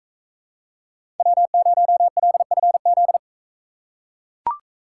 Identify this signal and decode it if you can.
Morse code, heard as a keyed tone, about 41 wpm, 700 Hz. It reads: W0LFZ